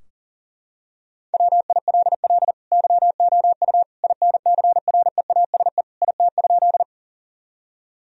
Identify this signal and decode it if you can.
Morse code, heard as a keyed tone, about 40 wpm, 690 Hz. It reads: WILL YOU INCREASE IT?
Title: Sentences from Top 1000 Words - Review Part 4 - 40wpm